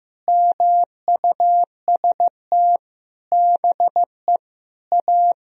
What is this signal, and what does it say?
Morse code, heard as a keyed tone, about 15 wpm, 700 Hz. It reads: MUST BE A
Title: Sentences from Top 1000 Words - Review Part 1 - 15wpm